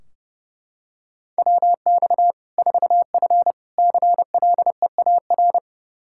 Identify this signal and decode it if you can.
Morse code, heard as a keyed tone, about 29 wpm, 700 Hz. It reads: WX 4F CLEAR
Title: Practice QSO - Exchange 2 - Info - 30wpm